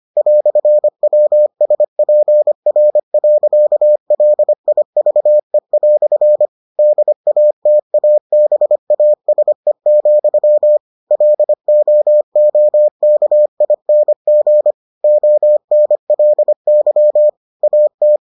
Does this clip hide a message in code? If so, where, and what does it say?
Yes: Morse code from 0.2 to 18.2 s, heard as a keyed tone, about 25 wpm, 600 Hz. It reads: "WSPR.LIVE" DATABASE, LOOKING ONLY AT